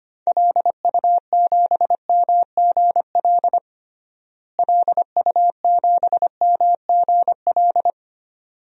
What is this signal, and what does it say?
Morse code, heard as a keyed tone, about 25 wpm, 700 Hz. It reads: LU7MGL LU7MGL
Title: DXCC-100 Call Signs - 6 Characters - Repeated 2x - 25wpm